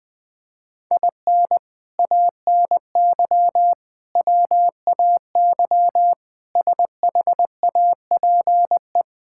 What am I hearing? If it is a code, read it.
Morse code, heard as a keyed tone, about 20 wpm, 700 Hz. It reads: IN ANY WAY SHAPE
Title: Sets of 6 Words - 20wpm